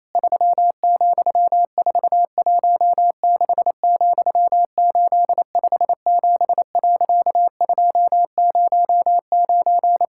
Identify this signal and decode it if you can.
Morse code, heard as a keyed tone, about 28 wpm, 700 Hz. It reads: 3,416,857.209
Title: Pangram Challenge v2 - 28wpm